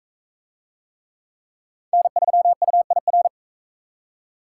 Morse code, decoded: N3UIR

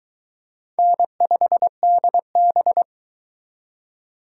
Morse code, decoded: N5DB

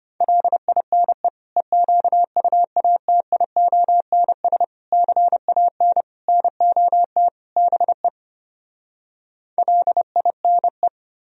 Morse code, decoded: LINE EQUATIONS CAN NOT BE LINE

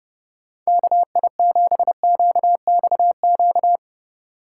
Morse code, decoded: KI7QXQ

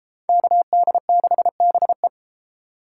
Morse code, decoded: KD6BE

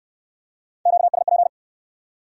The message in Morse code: QSP

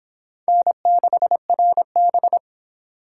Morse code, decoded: N6RB

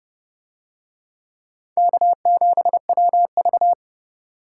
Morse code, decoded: K7WV

